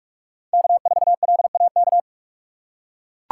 Morse code, heard as a keyed tone, about 44 wpm, 700 Hz. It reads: K4LAK
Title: USA Call Signs - 5 Characters - 4x Character Spacing - 45wpm